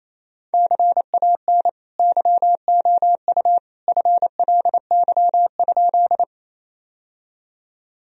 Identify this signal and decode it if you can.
Morse code, heard as a keyed tone, about 28 wpm, 700 Hz. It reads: CAN YOU FLY?
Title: Sentences from Top 1000 Words - Review Part 2 - 28wpm